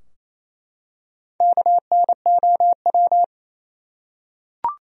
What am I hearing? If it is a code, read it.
Morse code, heard as a keyed tone, about 28 wpm, 700 Hz. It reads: KNOW
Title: Top 100 Words Repeated 28wpm